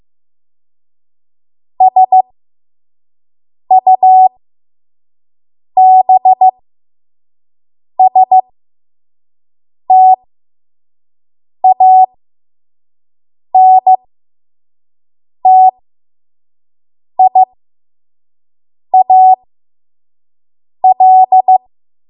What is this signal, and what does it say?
Morse code, heard as a keyed tone, about 15 wpm, spaced as for 5 wpm, 750 Hz. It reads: SUBSTANTIAL
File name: Morse / 05WPM